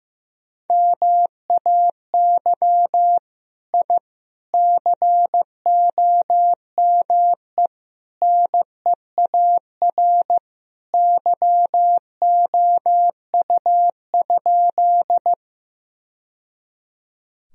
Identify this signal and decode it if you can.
Morse code, heard as a keyed tone, about 14 wpm, 700 Hz. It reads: MAY I COME NEAR YOU?
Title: Sentences from Top 1000 Words - Review Part 1 - 15wpm